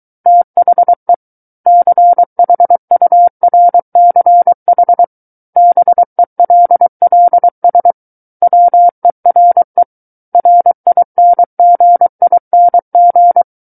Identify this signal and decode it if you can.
Morse code, heard as a keyed tone, about 23 wpm, 700 Hz. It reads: THE CHURCH BELLS WERE RINGING
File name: Book One / 14-Chapter 14 - 23 WPM